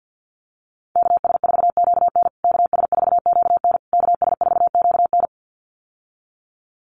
Morse code, heard as a keyed tone, about 48 wpm, 700 Hz. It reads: KS4QN KS4QN KS4QN
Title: USA Call Signs Repeated 3x 50wpm